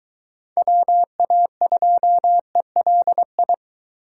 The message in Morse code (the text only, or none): WA2ELI